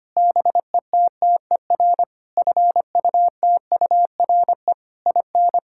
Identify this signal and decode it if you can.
Morse code, heard as a keyed tone, about 25 wpm, 700 Hz. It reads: BETTER FUTURE IN
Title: Sentences from Top 1000 Words - Review Part 6 - 25wpm